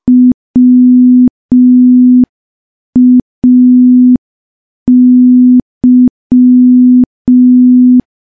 Morse code, heard as a keyed tone, about 5 wpm, 250 Hz. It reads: WAY
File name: morse / morse-training-05WPM